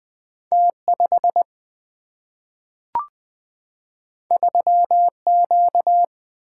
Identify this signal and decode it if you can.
Morse code, heard as a keyed tone, about 21 wpm, 700 Hz. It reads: T5 3Q
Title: Single-Letter Single-Number 20wpm